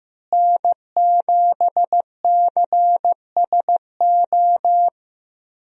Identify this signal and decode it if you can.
Morse code, heard as a keyed tone, about 15 wpm, 700 Hz. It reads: N7CSO
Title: USA Call Signs - 5 Characters - Repeated 3x - 15wpm